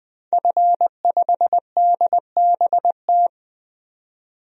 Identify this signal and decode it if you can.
Morse code, heard as a keyed tone, about 20 wpm, 700 Hz. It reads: F5DBT
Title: DXCC-100 Call Signs - 5 Characters - 20wpm